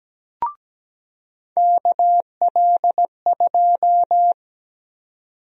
Morse code, heard as a keyed tone, about 17 wpm, 700 Hz. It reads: KL2
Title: USA Call Sign Prefixes - 17wpm